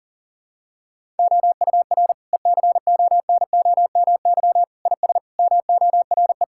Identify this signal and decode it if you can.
Morse code, heard as a keyed tone, about 37 wpm, 700 Hz. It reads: OUR ECONOMY IS MORE
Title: Sentences from Top 1000 Words - Review Part 6 - 40wpm